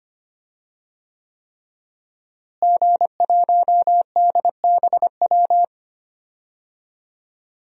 Morse code, heard as a keyed tone, about 25 wpm, 700 Hz. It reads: G1DBW
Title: DXCC-100 Call Signs - 5 Characters - Repeated 3x - 25wpm